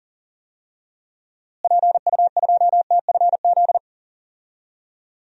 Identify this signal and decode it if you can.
Morse code, heard as a keyed tone, about 40 wpm, 700 Hz. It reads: PU2TFZ
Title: DXCC-100 Call Signs - 6 Characters - Repeated 2x - 40wpm